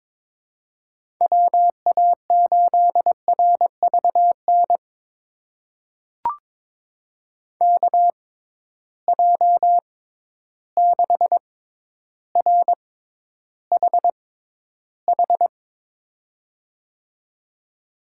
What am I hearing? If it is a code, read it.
Morse code, heard as a keyed tone, about 22 wpm, spaced as for 11 wpm, 700 Hz. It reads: WA8RVN KJ6RHH
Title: USA Call Signs - 6 Characters - 6x Character Spacing - 22wpm